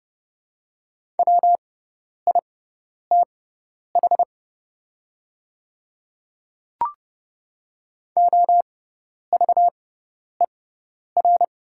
Morse code, heard as a keyed tone, about 30 wpm, spaced as for 10 wpm, 700 Hz. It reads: WITH OVER
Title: 6x Character Spacing - Top 100 Words - 30wpm